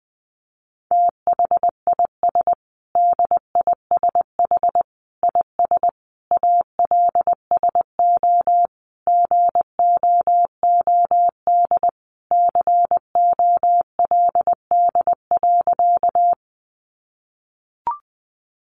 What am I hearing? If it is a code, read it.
Morse code, heard as a keyed tone, about 20 wpm, 700 Hz. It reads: THIS DISH IS ALSO GOOD COLD.
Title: Sentences from Top 1300 Words 20wpm